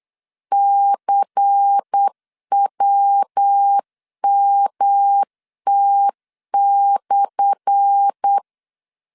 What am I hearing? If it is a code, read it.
Morse code, heard as a keyed tone, about 8 wpm, 800 Hz. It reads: CWMT/
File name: LEZIONI / 10 SIGNS / 10A SLASH 8wpm 3452